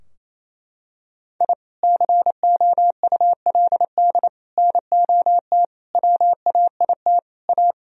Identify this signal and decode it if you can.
Morse code, heard as a keyed tone, about 28 wpm, 700 Hz. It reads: I COULD NOT WAIT A
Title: Sentences from Top 1000 Words - Review Part 4 - 28wpm